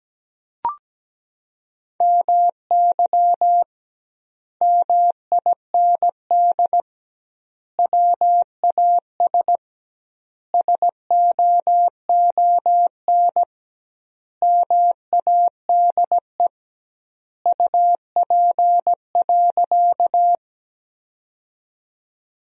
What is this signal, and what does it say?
Morse code, heard as a keyed tone, about 17 wpm, 700 Hz. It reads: MY MIND WAS SOON MADE UP.
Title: 2x Word-Spacing - Sentences from Top 500 Words - 17wpm